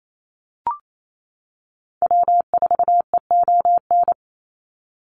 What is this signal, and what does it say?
Morse code, heard as a keyed tone, about 28 wpm, 700 Hz. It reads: W4EON
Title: NA SOTA Activator Callsigns 28wpm